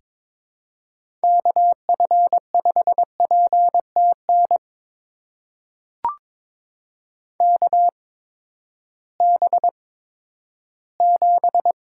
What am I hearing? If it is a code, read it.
Morse code, heard as a keyed tone, about 21 wpm, spaced as for 12 wpm, 700 Hz. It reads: KF5PTN KB7
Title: USA Call Signs - 6 Characters - 8x Character Spacing - 22wpm